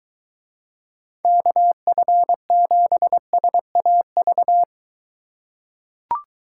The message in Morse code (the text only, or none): KF7SAV